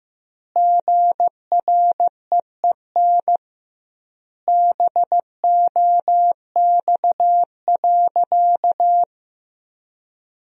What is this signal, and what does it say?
Morse code, heard as a keyed tone, about 15 wpm, 700 Hz. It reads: GREEN BOX.
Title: 2x Word-Spacing - Sentences from Top 500 Words - 15wpm